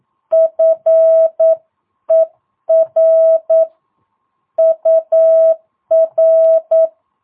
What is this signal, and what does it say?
Morse code, heard as a keyed tone, about 9 wpm, 650 Hz. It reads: FER UR
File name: jh8cbh / 2022cw